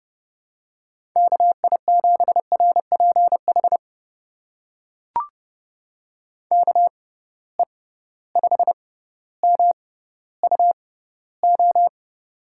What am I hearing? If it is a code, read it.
Morse code, heard as a keyed tone, about 30 wpm, spaced as for 15 wpm, 700 Hz. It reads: KI7RPH KE5MUO
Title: USA Call Signs - 6x Character Spacing - 30wpm